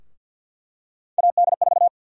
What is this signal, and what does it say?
Morse code, heard as a keyed tone, about 48 wpm, 700 Hz. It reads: AD4
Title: USA Call Sign Prefixes - 50wpm